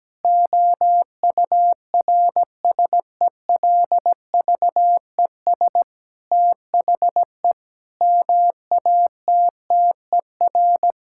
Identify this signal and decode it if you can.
Morse code, heard as a keyed tone, about 17 wpm, 700 Hz. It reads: OURSELVES THE MATTER
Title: Sentences from Top 1600 Words 17wpm